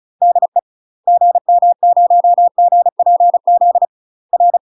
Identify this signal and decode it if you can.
Morse code, heard as a keyed tone, about 36 wpm, 700 Hz. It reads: DE GM0GPZ R
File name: UK MASTER QSOs TRAINING / 35WPM / UK MASTER QSOs 0010